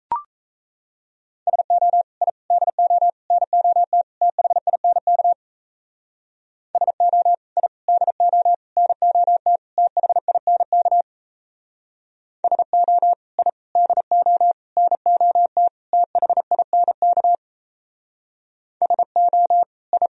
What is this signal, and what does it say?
Morse code, heard as a keyed tone, about 35 wpm, 700 Hz. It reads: SO I DO NOT THINK SO I DO NOT THINK SO I DO NOT THINK SO I
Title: Speed-Racing - Sets of 5 Words 28wpm